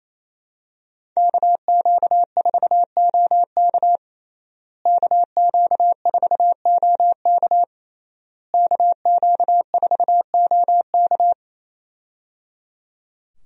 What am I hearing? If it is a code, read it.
Morse code, heard as a keyed tone, about 28 wpm, 700 Hz. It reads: KQ4OK KQ4OK KQ4OK
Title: USA Call Signs - 5 Characters - Repeated 3x - 28wpm